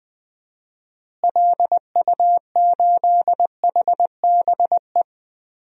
Morse code, heard as a keyed tone, about 20 wpm, 700 Hz. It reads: LU8HBE
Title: DXCC-100 Call Signs - 6 Characters - Repeated 3x - 20wpm